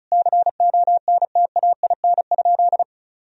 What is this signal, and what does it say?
Morse code, heard as a keyed tone, about 34 wpm, 700 Hz. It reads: CONTAIN?
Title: Sentences from Top 1000 Words - Review Part 2 - 35wpm